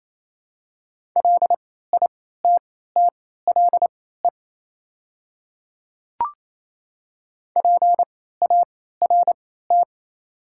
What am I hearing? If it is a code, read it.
Morse code, heard as a keyed tone, about 28 wpm, spaced as for 16 wpm, 700 Hz. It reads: LITTLE PART
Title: 3x Character Spacing - Top 200 Words - 28wpm